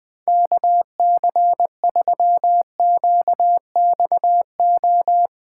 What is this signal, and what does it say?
Morse code, heard as a keyed tone, about 20 wpm, 700 Hz. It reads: KC3QXO
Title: 2023 POTA Activator Call Signs - 20wpm